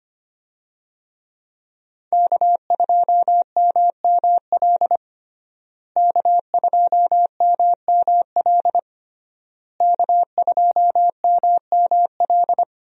Morse code, heard as a keyed tone, about 25 wpm, 700 Hz. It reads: K2MML K2MML K2MML
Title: USA Call Signs - 5 Characters - Repeated 3x - 25wpm